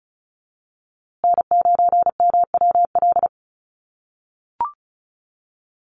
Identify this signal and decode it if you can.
Morse code, heard as a keyed tone, about 34 wpm, 700 Hz. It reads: N9MWL
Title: USA Call Signs Repeated 2x 35wpm